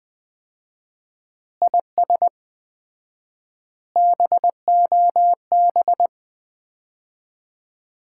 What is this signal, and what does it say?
Morse code, heard as a keyed tone, about 21 wpm, 700 Hz. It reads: IS BOB
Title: Common QSO Phrases - 4x Word-Spacing - 20wpm